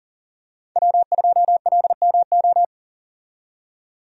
Morse code, heard as a keyed tone, about 40 wpm, 700 Hz. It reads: W2LMO
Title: USA Call Signs - 5 Characters - 40wpm